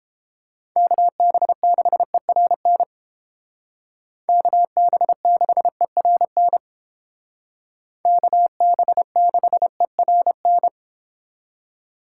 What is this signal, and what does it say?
Morse code, heard as a keyed tone, about 30 wpm, 700 Hz. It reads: KB6ERN KB6ERN KB6ERN